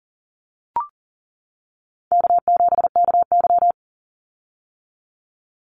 Morse code, heard as a keyed tone, about 38 wpm, 700 Hz. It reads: K7KY
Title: Easy Call Signs - 40wpm